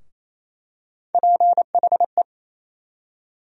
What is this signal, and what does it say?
Morse code, heard as a keyed tone, about 28 wpm, 700 Hz. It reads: PHE
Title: USA Call Sign Postfixes - 3 Letters 28wpm